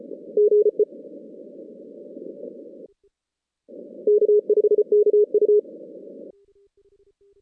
Morse code, heard as a keyed tone, about 34 wpm, 420 Hz. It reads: GE K5KU
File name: 2016 JAN CW NAQP / 2016 Jan naqp k5ot - 0-10